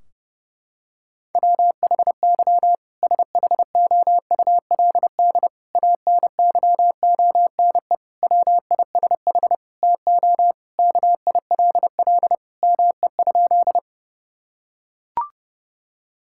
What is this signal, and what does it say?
Morse code, heard as a keyed tone, about 30 wpm, 700 Hz. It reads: WHY SHOULD ANYONE WISH TO KILL ME?
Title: Sentences from Top 1600 Words 30wpm